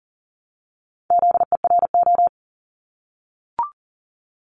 Morse code, heard as a keyed tone, about 40 wpm, 700 Hz. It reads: ZERO
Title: Multi-letter Phonemes Repeated 3x 40wpm